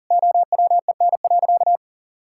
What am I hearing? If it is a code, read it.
Morse code, heard as a keyed tone, about 40 wpm, 700 Hz. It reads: OWEN.
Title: Pangram Challenge v2 - 40wpm